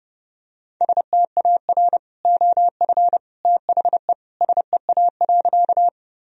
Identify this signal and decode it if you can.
Morse code, heard as a keyed tone, about 30 wpm, 700 Hz. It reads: STAR OF THE SEA.
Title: Sentences from Top 1000 Words - Review Part 2 - 30wpm